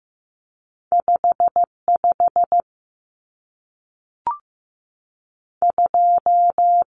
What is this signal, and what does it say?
Morse code, heard as a keyed tone, about 15 wpm, spaced as for 3 wpm, 700 Hz. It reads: 5 52
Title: Numbers - 2-Digit 15wpm